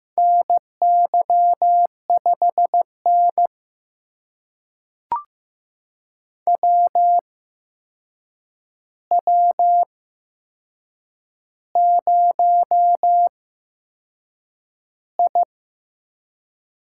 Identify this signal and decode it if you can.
Morse code, heard as a keyed tone, about 15 wpm, spaced as for 6 wpm, 700 Hz. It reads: NY5N WW0I
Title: USA Call Signs - 4 Characters - 8x Character Spacing - 15wpm